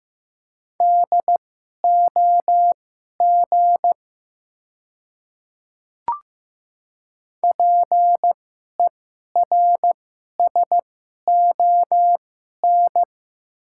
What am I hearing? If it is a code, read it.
Morse code, heard as a keyed tone, about 15 wpm, spaced as for 11 wpm, 700 Hz. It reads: DOG PERSON
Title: 2x Character Spacing - Top 500 Words - 15wpm